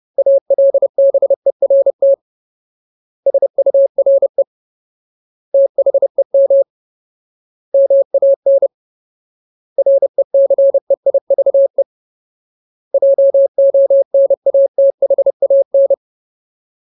Morse code, heard as a keyed tone, about 30 wpm, 550 Hz. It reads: ALBERT SURE THEM MAN RECEIVE JONATHAN
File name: Ultimate Word List / 30WPM Ultimate